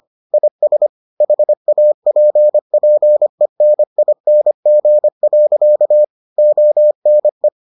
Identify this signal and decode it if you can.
Morse code, heard as a keyed tone, about 25 wpm, 600 Hz. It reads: IS HAPPENING. ONE